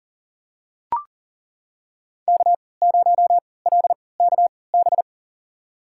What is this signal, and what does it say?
Morse code, heard as a keyed tone, about 38 wpm, spaced as for 22 wpm, 700 Hz. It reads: K0LKB
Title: USA Call Signs - 5 Characters - 3x Character Spacing - 40wpm